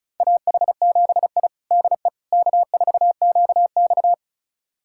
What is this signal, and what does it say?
Morse code, heard as a keyed tone, about 36 wpm, 700 Hz. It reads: AH7I DE K4QX
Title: USA Callsign DE Callsign - 4 Characters - 35wpm